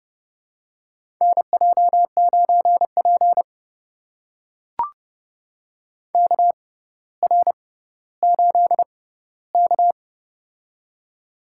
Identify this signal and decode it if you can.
Morse code, heard as a keyed tone, about 32 wpm, spaced as for 16 wpm, 700 Hz. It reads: NJ9P KR8K
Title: USA Call Signs - 4 Characters - 6x Character Spacing - 30wpm